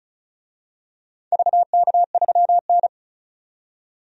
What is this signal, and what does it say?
Morse code, heard as a keyed tone, about 34 wpm, 700 Hz. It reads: VK3N
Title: DXCC-100 Call Signs - 4 Characters - 35wpm